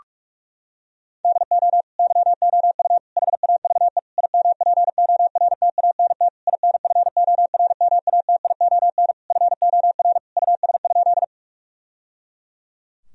Morse code, heard as a keyed tone, about 41 wpm, 700 Hz. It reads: DO YOU HAVE IMPORTANT INFORMATION FOR US?
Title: Sentences from Top 1000 Words - Review Part 3 - 45wpm